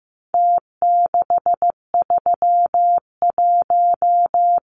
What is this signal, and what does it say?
Morse code, heard as a keyed tone, about 15 wpm, 700 Hz. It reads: T631
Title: Numbers - Contest Style - 4-Digit 15wpm